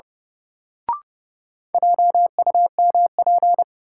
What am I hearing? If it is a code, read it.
Morse code, heard as a keyed tone, about 31 wpm, 700 Hz. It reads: JUMP
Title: Sentences from Top 1300 Words - New Word Drill Alt - 30wpm